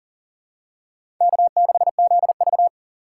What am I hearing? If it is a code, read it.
Morse code, heard as a keyed tone, about 40 wpm, 700 Hz. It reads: K6ZV
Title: USA Call Signs - 4 Characters - 4x Character Spacing - 40wpm